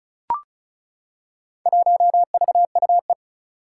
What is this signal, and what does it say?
Morse code, heard as a keyed tone, about 36 wpm, 690 Hz. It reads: 1VUE
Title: Four Challenging Letter-Number Combinations 35wpm